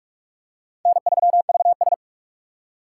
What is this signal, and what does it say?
Morse code, heard as a keyed tone, about 42 wpm, 700 Hz. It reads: N3VS
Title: USA Call Signs - 4 Characters - 3x Character Spacing - 45wpm